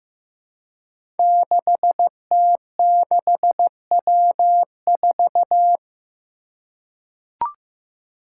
Morse code, heard as a keyed tone, about 15 wpm, 700 Hz. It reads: 6T6W4
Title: Five Challenging Letter-Number Combinations 15wpm